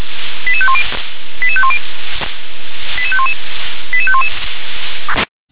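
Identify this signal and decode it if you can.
Five-tone selective call (ZVEI), heard as quick runs of five tones, tones of about 70 ms. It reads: 88510 88510 88511 88511
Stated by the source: ZVEI five-tone call 88753 received by SDR.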